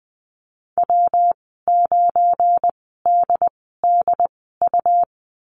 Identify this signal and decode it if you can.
Morse code, heard as a keyed tone, about 20 wpm, spaced as for 15 wpm, 700 Hz. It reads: W9DDU
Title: 179 Lesson 29g - Slash - Call Signs 20wpm